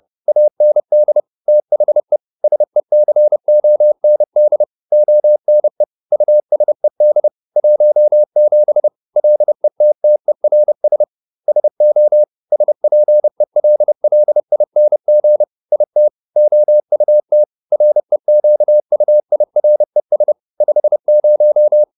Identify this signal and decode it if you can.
Morse code, heard as a keyed tone, about 30 wpm, 600 Hz. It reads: AND THE SECOND ONE USED 17 LETTERS SO SPELLING IT OUT REQUIRES 50